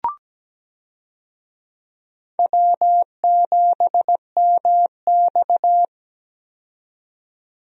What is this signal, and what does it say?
Morse code, heard as a keyed tone, about 17 wpm, 700 Hz. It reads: W7MX
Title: USA Call Signs - 4 Characters - 17wpm